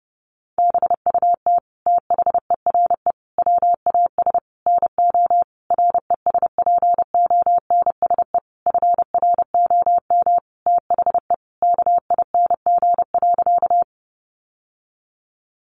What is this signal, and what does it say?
Morse code, heard as a keyed tone, about 30 wpm, 700 Hz. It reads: BUT THERE WAS NO RESPONSE FROM THE KING.
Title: Sentences from Top 1400 Words 30wpm